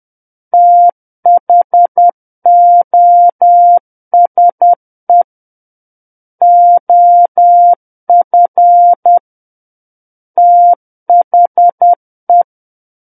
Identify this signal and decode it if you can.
Morse code, heard as a keyed tone, about 10 wpm, 700 Hz. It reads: THOSE OF THE
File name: Book One / 01-Chapter 1 - 10 WPM